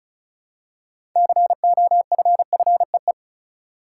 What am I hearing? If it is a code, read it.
Morse code, heard as a keyed tone, about 36 wpm, 700 Hz. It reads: COFFEE
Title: Six Letter Words - 35wpm